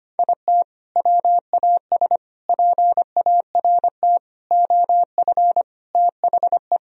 Morse code, heard as a keyed tone, about 25 wpm, 700 Hz. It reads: IT WAS PART OF THE
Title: Sentences from Top 1000 Words - Review Part 4 - 25wpm